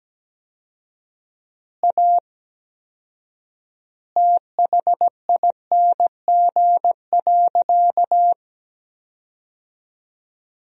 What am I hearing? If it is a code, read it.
Morse code, heard as a keyed tone, about 17 wpm, 700 Hz. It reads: A THING.
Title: 4x Word-Spacing - Sentences from Top 200 Words - 17wpm